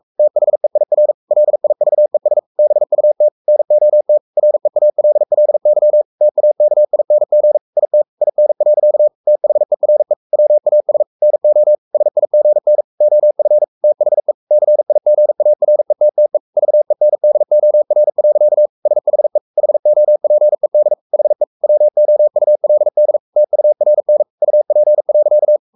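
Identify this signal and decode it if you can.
Morse code, heard as a keyed tone, about 44 wpm, 600 Hz. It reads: THEIR LIVES BUT NOT REALLY TAKING IT IN. THERE WAS NO SIGN OF THE CIGARETTE VENDOR. SHE HOPED HE WOULD TURN UP.
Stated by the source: Morse code at 43 wpm sending Ken Follett's novel Never in English, from CW med Gnister.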